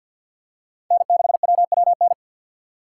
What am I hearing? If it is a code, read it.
Morse code, heard as a keyed tone, about 52 wpm, 700 Hz. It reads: N6WWN
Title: USA Call Signs - 5 Characters - 2x Character Spacing - 50wpm